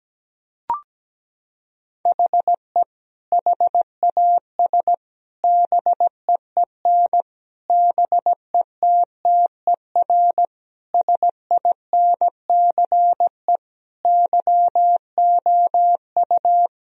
Morse code, heard as a keyed tone, about 17 wpm, 700 Hz. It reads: HE HAS BEEN BETTER SINCE YOU
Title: Sentences from Top 1000 Words - Review Part 1 - 17wpm